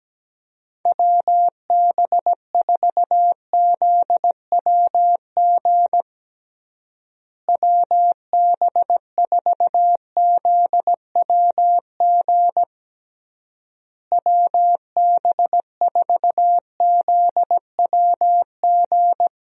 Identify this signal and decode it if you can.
Morse code, heard as a keyed tone, about 17 wpm, 700 Hz. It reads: WB4ZWG WB4ZWG WB4ZWG
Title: USA Call Signs - 6 Characters - Repeated 3x - 17wpm